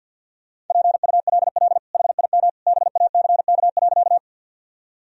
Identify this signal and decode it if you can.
Morse code, heard as a keyed tone, about 53 wpm, 700 Hz. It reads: PULL HIM BACK.
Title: Sentences from Top 1000 Words - Review Part 2 - 50wpm